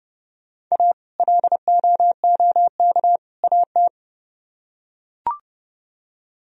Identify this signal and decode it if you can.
Morse code, heard as a keyed tone, about 29 wpm, 700 Hz. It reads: A LOOK AT